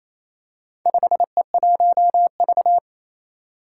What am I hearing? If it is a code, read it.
Morse code, heard as a keyed tone, about 29 wpm, 700 Hz. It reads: 5E1V
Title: Four Challenging Letter-Number Combinations 28wpm